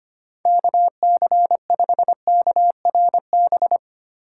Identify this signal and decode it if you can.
Morse code, heard as a keyed tone, about 25 wpm, 700 Hz. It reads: KC5KRB
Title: USA Call Signs - 6 Characters - 3x Character Spacing - 25wpm